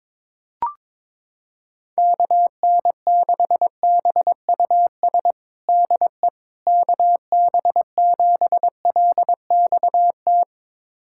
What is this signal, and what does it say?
Morse code, heard as a keyed tone, about 22 wpm, 700 Hz. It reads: KN6BUS DE KB7LXT
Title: USA Callsign DE Callsign - 6 Characters - 22wpm